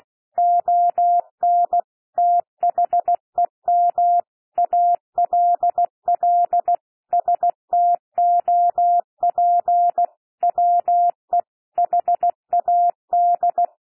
Morse code, heard as a keyed tone, about 16 wpm, 700 Hz. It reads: ON THEM ALL STOP WE HAD
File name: cwmp3 / P1 SOUND PASSAGE 16 wpm